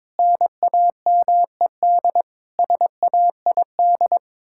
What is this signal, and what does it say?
Morse code, heard as a keyed tone, about 22 wpm, 700 Hz. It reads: NAMED SAID
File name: practice / 22wpm / Sets of 7 Words - 22wpm